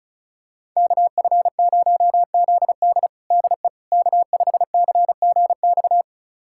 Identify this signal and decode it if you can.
Morse code, heard as a keyed tone, about 36 wpm, 700 Hz. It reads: KF0ZD DE K5CGX
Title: USA Callsign DE Callsign - 5 Characters - 35wpm